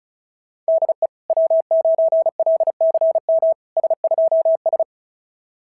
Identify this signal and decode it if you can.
Morse code, heard as a keyed tone, about 33 wpm, 690 Hz. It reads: DE W9LCM S2S
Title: SOTA QSO Practice 35wpm